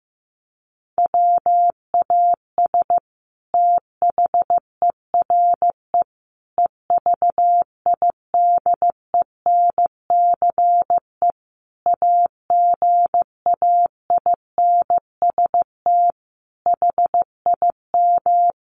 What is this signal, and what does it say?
Morse code, heard as a keyed tone, about 15 wpm, 700 Hz. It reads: WAS THERE EVIDENCE AGAINST HIM
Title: Sentences from Top 1300 Words 15wpm